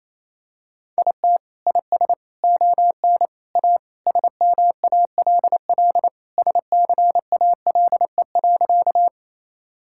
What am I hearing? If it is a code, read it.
Morse code, heard as a keyed tone, about 28 wpm, 700 Hz. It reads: IT IS ON A SMALL SCALE.
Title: Sentences from Top 1000 Words - Review Part 5 - 28wpm